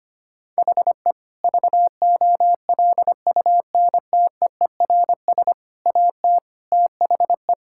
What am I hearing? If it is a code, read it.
Morse code, heard as a keyed tone, about 25 wpm, 700 Hz. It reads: HE VOLUNTEERS AT THE
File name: practice / 25wpm / Difficult Sentences with Hints - 25wpm